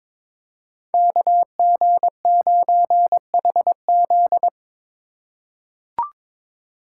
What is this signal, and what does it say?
Morse code, heard as a keyed tone, about 22 wpm, 700 Hz. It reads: KG9HZ
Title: USA Call Signs - 5 Characters - 4x Character Spacing - 22wpm